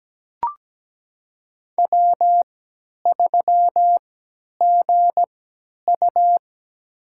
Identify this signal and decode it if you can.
Morse code, heard as a keyed tone, about 17 wpm, spaced as for 10 wpm, 700 Hz. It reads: W3GU
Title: USA Call Signs - 3x Character Spacing - 17wpm